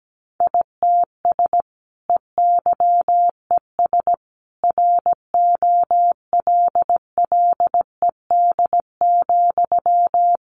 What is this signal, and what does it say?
Morse code, heard as a keyed tone, about 17 wpm, 700 Hz. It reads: ITS EYES ROLLED,
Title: A Sound Of Thunder 17wpm